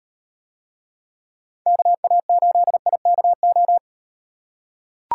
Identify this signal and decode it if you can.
Morse code, heard as a keyed tone, about 38 wpm, 700 Hz. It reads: KA8IKO